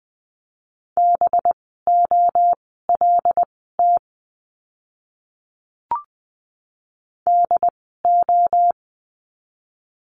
Morse code, heard as a keyed tone, about 20 wpm, spaced as for 15 wpm, 700 Hz. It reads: BOLT DO
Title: 185 Lesson 30f - Letter b - Words 20wpm